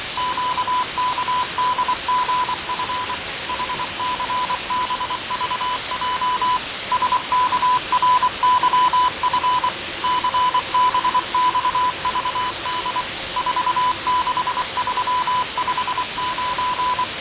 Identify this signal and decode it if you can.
Morse code, heard as a keyed tone, about 24 wpm, 1000 Hz, with radio noise. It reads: QKDGF HCBVJ SKRYF CBXVD 46359
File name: fav22-6824usb-20251029-1420z-voice-test-win10